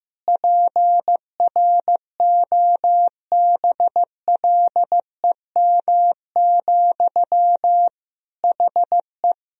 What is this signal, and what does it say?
Morse code, heard as a keyed tone, about 15 wpm, 700 Hz. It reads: PROBLEM, HE
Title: Sentences from Top 1000 Words - Review Part 2 - 15wpm